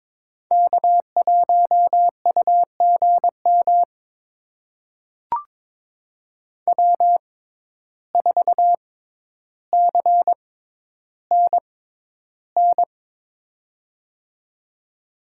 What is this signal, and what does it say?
Morse code, heard as a keyed tone, about 22 wpm, spaced as for 11 wpm, 700 Hz. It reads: K1UGM W4CNN
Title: USA Call Signs - 5 Characters - 6x Character Spacing - 22wpm